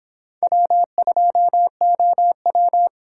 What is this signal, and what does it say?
Morse code, heard as a keyed tone, about 26 wpm, 700 Hz. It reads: W2OW